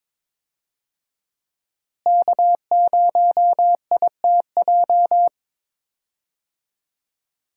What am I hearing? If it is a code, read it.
Morse code, heard as a keyed tone, about 21 wpm, 700 Hz. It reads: K0ITJ
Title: USA Call Signs - 5 Characters - Repeated 3x - 22wpm